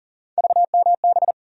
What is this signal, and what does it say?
Morse code, heard as a keyed tone, about 40 wpm, 700 Hz. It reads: VMB
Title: USA Call Sign Postfixes - 3 Letters 40wpm